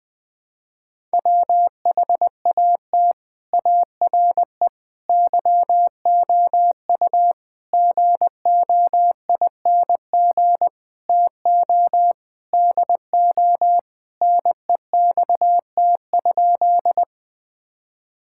Morse code, heard as a keyed tone, about 20 wpm, 700 Hz. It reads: WHAT ARE YOU GOING TO DO NEXT?
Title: Sentences with 5-Word Phrases - 20wpm